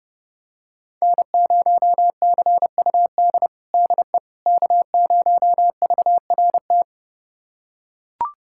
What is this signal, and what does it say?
Morse code, heard as a keyed tone, about 32 wpm, 700 Hz. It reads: N0CUD DE K0VRT
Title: USA Callsign DE Callsign - 5 Characters - 30wpm